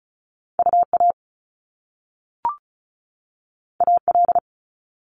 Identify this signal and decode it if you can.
Morse code, heard as a keyed tone, about 36 wpm, 700 Hz. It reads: UA AL
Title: Top Two Letter Combinations in Words 35wpm